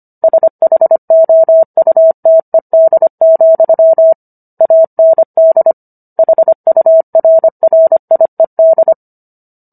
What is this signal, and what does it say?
Morse code, heard as a keyed tone, about 25 wpm, 650 Hz. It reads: SHOUTED, AND HURRIED